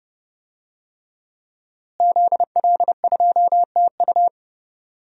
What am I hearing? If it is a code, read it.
Morse code, heard as a keyed tone, about 32 wpm, 700 Hz. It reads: ZL2TU
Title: DXCC-100 Call Signs - 5 Characters - Repeated 3x - 30wpm